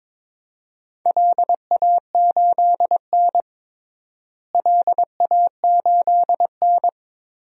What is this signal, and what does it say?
Morse code, heard as a keyed tone, about 21 wpm, 700 Hz. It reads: LA8N LA8N
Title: DXCC-100 Call Signs - 4 Characters - Repeated 3x - 22wpm